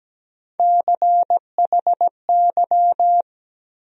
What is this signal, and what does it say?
Morse code, heard as a keyed tone, about 17 wpm, 700 Hz. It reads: CHY
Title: USA Call Sign Postfixes - 3 Letters 17wpm